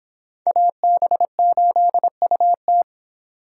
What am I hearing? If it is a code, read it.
Morse code, heard as a keyed tone, about 25 wpm, 700 Hz. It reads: AB8UT